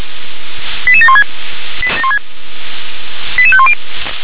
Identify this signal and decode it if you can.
Five-tone selective call (ZVEI), heard as quick runs of five tones, tones of about 70 ms. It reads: 88716 88510